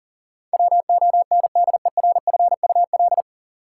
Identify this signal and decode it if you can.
Morse code, heard as a keyed tone, about 38 wpm, 700 Hz. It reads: WONDERFUL
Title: Sentences from Top 1100 Words - New Word Drill Alt - 40wpm